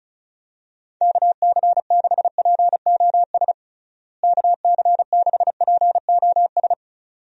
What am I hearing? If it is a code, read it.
Morse code, heard as a keyed tone, about 34 wpm, 700 Hz. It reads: KC6POS KC6POS
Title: USA Call Signs - 6 Characters - Repeated 2x - 35wpm